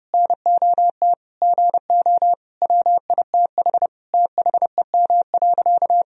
Morse code, heard as a keyed tone, about 30 wpm, 700 Hz. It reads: NOT GO WITH THEM.
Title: Sentences from Top 1000 Words - Review Part 4 - 30wpm